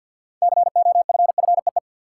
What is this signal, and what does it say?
Morse code, heard as a keyed tone, about 46 wpm, 700 Hz. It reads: COFFEE
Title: Six Letter Words - 50wpm